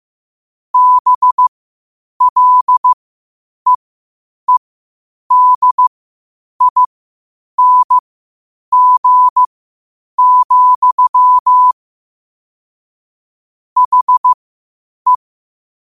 Morse code, heard as a keyed tone, about 15 wpm, spaced as for 8 wpm, 1000 Hz. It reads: BLEEDING, HE
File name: qotdfiles / 8wpm